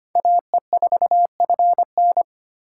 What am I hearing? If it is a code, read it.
Morse code, heard as a keyed tone, about 32 wpm, 700 Hz. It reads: AE4FN